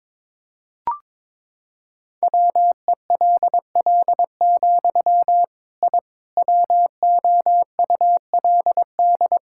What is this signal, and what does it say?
Morse code, heard as a keyed tone, about 22 wpm, 700 Hz. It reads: WELL, I WOULD
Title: Sentences from Top 1000 Words - Review Part 5 - 22wpm